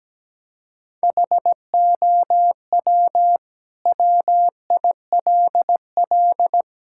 Morse code, heard as a keyed tone, about 17 wpm, 700 Hz. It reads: HOW WILL
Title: Sentences from Top 1000 Words - Review Part 5 - 17wpm